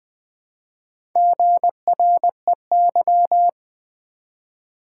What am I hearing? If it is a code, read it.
Morse code, heard as a keyed tone, about 20 wpm, 700 Hz. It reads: GREY
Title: Build A Word - 1 Letter At A Time - 3 to 5-Letter Words - 20wpm